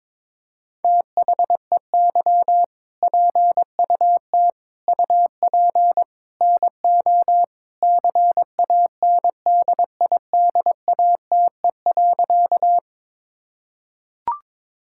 Morse code, heard as a keyed tone, about 22 wpm, 700 Hz. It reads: THEY PUT UP NO CANDIDATE.
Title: Sentences from Top 1600 Words 22wpm